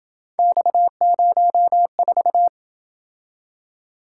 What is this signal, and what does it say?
Morse code, heard as a keyed tone, about 27 wpm, 700 Hz. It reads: X04